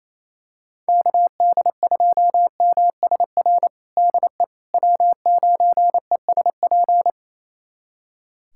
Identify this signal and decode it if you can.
Morse code, heard as a keyed tone, about 28 wpm, 700 Hz. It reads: KD2MSR DE W9ESP
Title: USA Callsign DE Callsign 28wpm